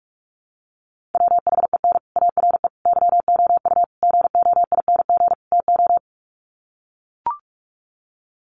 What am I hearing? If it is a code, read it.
Morse code, heard as a keyed tone, about 44 wpm, 700 Hz. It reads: WHEN ARE YOU GOING TO
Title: Sets of 5 Words Encore2 45wpm